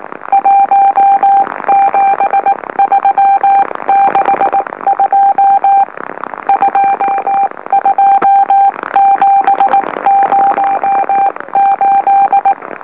Morse code, heard as a keyed tone, about 19 wpm, 790 Hz. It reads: 17362 22708
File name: M04- -2345utc-LOLO